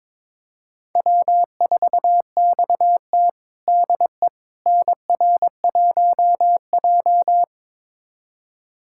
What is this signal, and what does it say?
Morse code, heard as a keyed tone, about 21 wpm, 700 Hz. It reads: W4XT DE NR1J
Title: USA Callsign DE Callsign - 4 Characters - 22wpm